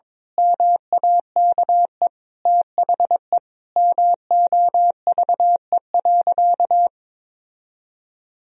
Morse code, heard as a keyed tone, about 22 wpm, 700 Hz. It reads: MAKE THE MOVE.
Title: Sentences from Top 1000 Words - Review Part 1 - 22wpm